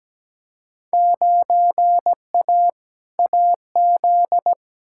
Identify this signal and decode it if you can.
Morse code, heard as a keyed tone, about 17 wpm, 700 Hz. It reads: 9A AZ
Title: Field Day Exchanges - 17wpm